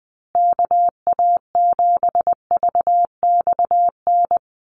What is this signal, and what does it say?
Morse code, heard as a keyed tone, about 20 wpm, 700 Hz. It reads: KA7VXN
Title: USA Call Signs Repeated 3x 20wpm